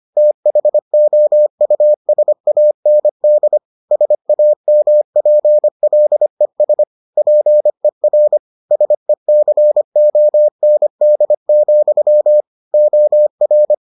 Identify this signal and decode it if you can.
Morse code, heard as a keyed tone, about 25 wpm, 600 Hz. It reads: THOUSAND SAMPLES PER SECOND, OR